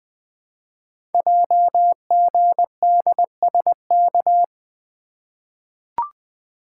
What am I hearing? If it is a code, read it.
Morse code, heard as a keyed tone, about 20 wpm, 700 Hz. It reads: JGDSK
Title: Code Groups - 5 Letter:Numbers - 20wpm